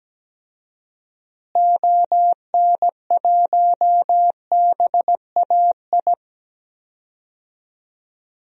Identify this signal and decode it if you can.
Morse code, heard as a keyed tone, about 17 wpm, 700 Hz. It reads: ON1BAI
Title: DXCC-100 Call Signs - 6 Characters - Repeated 2x - 17wpm